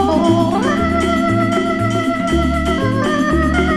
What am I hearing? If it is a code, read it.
There is no code.